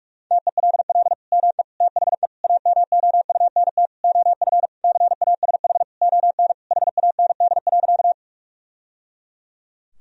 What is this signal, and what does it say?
Morse code, heard as a keyed tone, about 41 wpm, 700 Hz. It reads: TELL ME THE AMOUNT OF CASH ON HAND.
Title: Sentences from Top 1000 Words - Review Part 4 - 45wpm